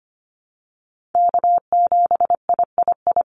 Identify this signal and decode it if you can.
Morse code, heard as a keyed tone, about 25 wpm, 700 Hz. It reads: K7III
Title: NA SOTA Chaser Callsigns 25wpm